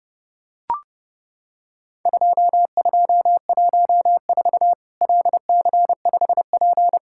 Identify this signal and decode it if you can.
Morse code, heard as a keyed tone, about 30 wpm, 700 Hz. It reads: 2214 LC5P
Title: Worked All Europe DX Contest - QTC Exchange - 30wpm